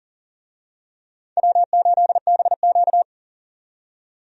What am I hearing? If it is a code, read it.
Morse code, heard as a keyed tone, about 40 wpm, 700 Hz. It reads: W8BQ
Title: USA Call Signs - 4 Characters - 40wpm